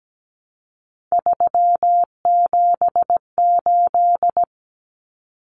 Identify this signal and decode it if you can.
Morse code, heard as a keyed tone, about 17 wpm, 700 Hz. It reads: 378
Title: Common Cut Numbers - 3-Digit 17wpm